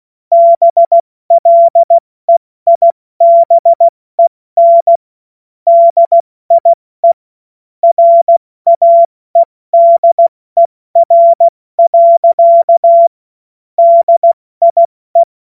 Morse code, heard as a keyed tone, about 16 wpm, 680 Hz. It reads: BLEIBEN DIE RAEDER. DIE
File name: Morsekurs / Witze / Witz 21. Juni 2022 70 F80